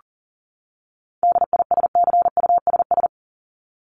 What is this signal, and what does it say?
Morse code, heard as a keyed tone, about 40 wpm, 700 Hz. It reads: DISCUSS
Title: ED Suffix - Top 100 Words with Base Words 40wpm